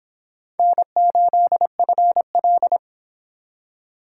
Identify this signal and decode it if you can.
Morse code, heard as a keyed tone, about 30 wpm, 700 Hz. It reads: N8FL